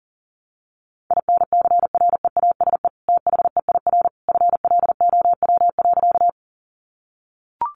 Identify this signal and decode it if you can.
Morse code, heard as a keyed tone, about 40 wpm, 700 Hz. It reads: INCREASE THEIR FLOW.
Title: Sentences from Top 1200 Words 40wpm